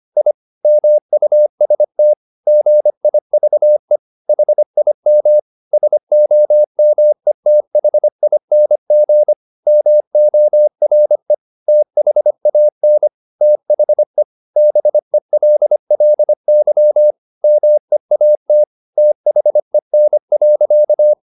Morse code, heard as a keyed tone, about 25 wpm, 600 Hz. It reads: I MUST GIVE HIM SOMETHING MORE THAN THE BELLY MEAT THEN.
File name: the old man and the sea part 1 25wpm